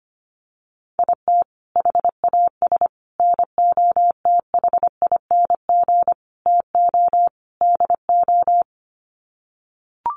Morse code, heard as a keyed tone, about 25 wpm, 700 Hz. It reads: IT HAS NOTHING TO DO
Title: Sets of 5 Words Encore 25wpm